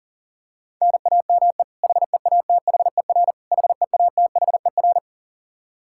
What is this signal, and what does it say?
Morse code, heard as a keyed tone, about 37 wpm, 700 Hz. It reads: NAME HEATHER HEATHER
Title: K1USN SST QSO Practice 40wpm